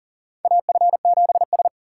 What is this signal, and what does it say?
Morse code, heard as a keyed tone, about 37 wpm, 700 Hz. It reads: AF7S
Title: USA Call Signs - 2x Character Spacing - 40wpm